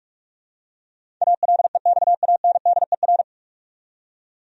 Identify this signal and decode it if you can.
Morse code, heard as a keyed tone, about 46 wpm, 700 Hz. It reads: ALEXANDER
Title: USA Common Names - 45wpm